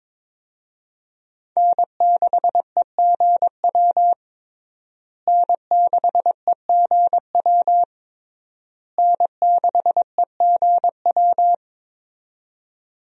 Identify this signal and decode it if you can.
Morse code, heard as a keyed tone, about 22 wpm, 700 Hz. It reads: N6EGW N6EGW N6EGW
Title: USA Call Signs - 5 Characters - Repeated 3x - 22wpm